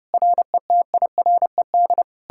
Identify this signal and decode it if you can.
Morse code, heard as a keyed tone, about 40 wpm, 700 Hz. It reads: RETIRED